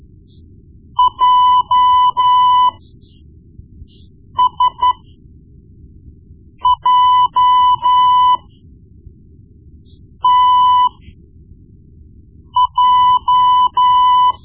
Morse code, heard as a keyed tone, about 13 wpm, spaced as for 4 wpm, 1000 Hz. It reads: JSJTJ
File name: Morse Lessons / Morse Lessons / Tape-1B